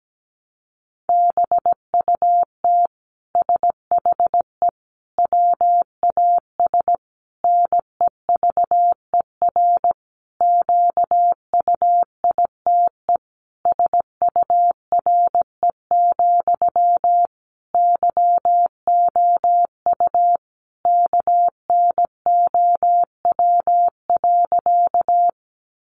Morse code, heard as a keyed tone, about 17 wpm, 700 Hz. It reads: BUT SHE WAS NEVER QUITE SURE, YOU KNOW.
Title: Sentences from Top 700 Words 17wpm